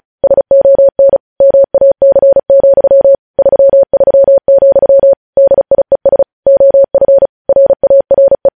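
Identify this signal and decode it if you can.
Morse code, heard as a keyed tone, about 36 wpm, 550 Hz. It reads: SON MAC, 33, DIES OF RARE
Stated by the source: Morse news headlines practice at 35 words per minute.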